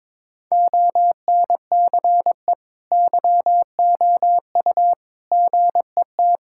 Morse code, heard as a keyed tone, about 22 wpm, 700 Hz. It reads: ONCE YOU GET
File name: practice / 22wpm / Sets of 7 Words - 22wpm